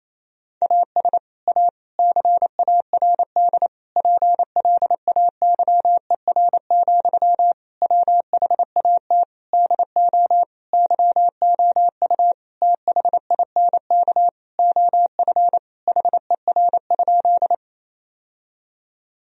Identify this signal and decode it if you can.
Morse code, heard as a keyed tone, about 28 wpm, 700 Hz. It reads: AS A CARD PLAYER, WHAT DO YOU THINK OF HER?
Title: Sentences from Top 1000 Words - Review Part 4 - 28wpm